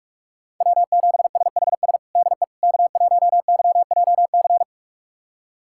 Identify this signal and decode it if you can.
Morse code, heard as a keyed tone, about 46 wpm, 700 Hz. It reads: W7SHS DE K1YJC
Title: USA Callsign DE Callsign - 5 Characters - 45wpm